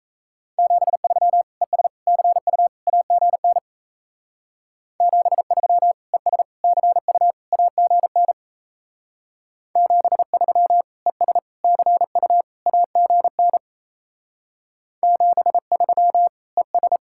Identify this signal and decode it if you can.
Morse code, heard as a keyed tone, about 37 wpm, 700 Hz. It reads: 73 ES CU AGN 73 ES CU AGN 73 ES CU AGN 73 ES